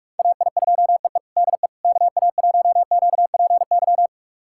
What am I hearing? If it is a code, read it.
Morse code, heard as a keyed tone, about 42 wpm, 700 Hz. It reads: AI2EE DE KA1QPY
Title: USA Callsign DE Callsign 45wpm